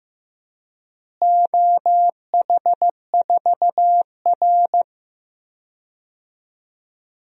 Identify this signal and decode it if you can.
Morse code, heard as a keyed tone, about 15 wpm, 700 Hz. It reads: OH4R
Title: DXCC-100 Call Signs - 4 Characters - Repeated 3x - 15wpm